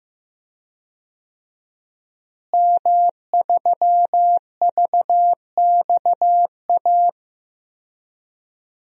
Morse code, heard as a keyed tone, about 15 wpm, 700 Hz. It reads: M3VXA
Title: DXCC-100 Call Signs - 5 Characters - Repeated 2x - 15wpm